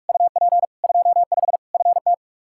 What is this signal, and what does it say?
Morse code, heard as a keyed tone, about 41 wpm, 700 Hz. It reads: UP 25 FT
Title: Practice QSO - Exchange 2 - Info - 45wpm